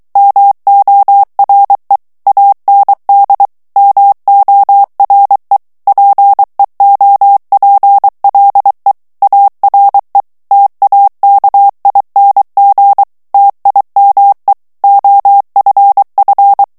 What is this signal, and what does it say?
Morse code, heard as a keyed tone, about 23 wpm, 800 Hz. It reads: MORE AND MORE PEOPLE ARE TAKING TIME OFF
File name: morse mp3 / ANGLAIS texte clair 1 vitesse 22 mots minute